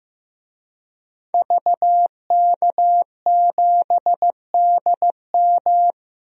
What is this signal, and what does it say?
Morse code, heard as a keyed tone, about 15 wpm, 700 Hz. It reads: VK7DM
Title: DXCC-100 Call Signs - 5 Characters - Repeated 2x - 15wpm